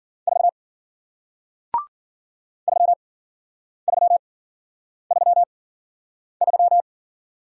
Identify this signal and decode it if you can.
Morse code, heard as a keyed tone, about 53 wpm, spaced as for 9 wpm, 700 Hz. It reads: 4 3333